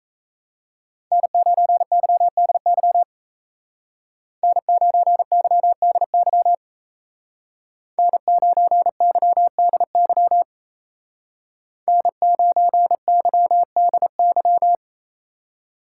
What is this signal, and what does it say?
Morse code, heard as a keyed tone, about 34 wpm, 700 Hz. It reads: N9YDY N9YDY N9YDY N9YDY
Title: Speed Racing - USA 1x3 Call Signs 28wpm